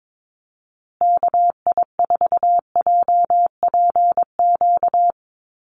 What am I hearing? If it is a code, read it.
Morse code, heard as a keyed tone, about 22 wpm, 700 Hz. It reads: KI4JPQ
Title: USA Call Signs Repeated 3x 22wpm